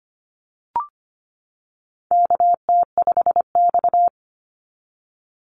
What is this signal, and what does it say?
Morse code, heard as a keyed tone, about 25 wpm, 700 Hz. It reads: KT5X
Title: NA SOTA Activator Callsigns 25wpm